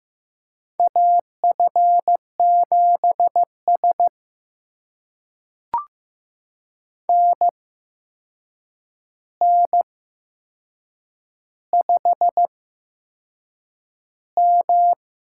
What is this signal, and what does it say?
Morse code, heard as a keyed tone, about 15 wpm, spaced as for 6 wpm, 700 Hz. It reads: AF7S NN5M
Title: USA Call Signs - 4 Characters - 8x Character Spacing - 15wpm